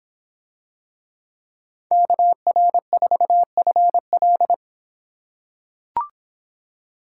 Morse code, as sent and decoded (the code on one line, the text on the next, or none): -.- .-. ....- ..-. .-..
KR4FL